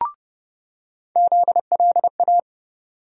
Morse code, heard as a keyed tone, about 30 wpm, 700 Hz. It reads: ZLA
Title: USA Call Sign Postfixes - 3 Letters 30wpm